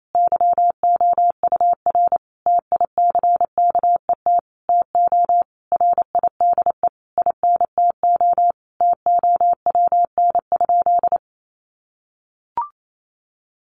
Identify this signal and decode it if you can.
Morse code, heard as a keyed tone, about 28 wpm, 700 Hz. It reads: YOUR TICKET TO RIDE INTO TOWN?
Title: Sentences from Top 1400 Words 28wpm